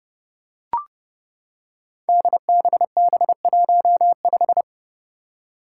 Morse code, heard as a keyed tone, about 30 wpm, 700 Hz. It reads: DBB15
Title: Five Challenging Letter-Number Combinations 30wpm